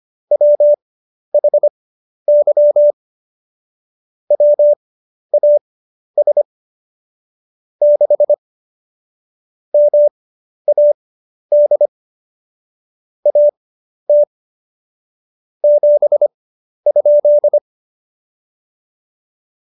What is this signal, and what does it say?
Morse code, heard as a keyed tone, about 25 wpm, spaced as for 11 wpm, 590 Hz. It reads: WHY WAS 6 MAD AT 7?
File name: Jokes-25-char-10-eff